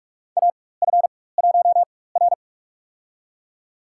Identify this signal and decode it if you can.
Morse code, heard as a keyed tone, about 44 wpm, spaced as for 21 wpm, 690 Hz. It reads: AF1R
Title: USA Call Signs - 4x Character Spacing - 45wpm